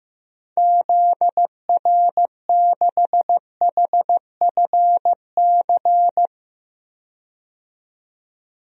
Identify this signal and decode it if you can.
Morse code, heard as a keyed tone, about 14 wpm, 700 Hz. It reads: ZR6HFC
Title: DXCC-100 Call Signs - 6 Characters - 15wpm